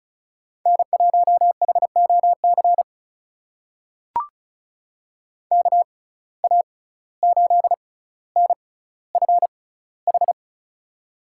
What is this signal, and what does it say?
Morse code, heard as a keyed tone, about 36 wpm, spaced as for 17 wpm, 700 Hz. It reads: N1HOC KA8NFH
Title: USA Call Signs - 6x Character Spacing - 35wpm